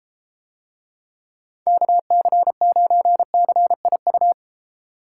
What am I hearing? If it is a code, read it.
Morse code, heard as a keyed tone, about 31 wpm, 700 Hz. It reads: KC9CIU